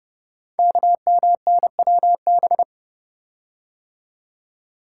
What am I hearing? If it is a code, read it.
Morse code, heard as a keyed tone, about 30 wpm, 700 Hz. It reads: KMNWB
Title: Code Groups - 5 Letters - 30wpm